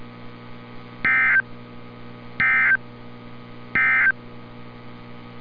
Emergency Alert System (SAME) tones, heard as harsh buzzing bursts. It is heard from 0.9 to 4.2 s.
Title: NOAA Weather Radio - Altoona, PA